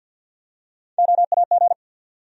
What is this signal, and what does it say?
Morse code, heard as a keyed tone, about 53 wpm, 700 Hz. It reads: OAG
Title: USA Call Sign Postfixes - 3 Letters 50wpm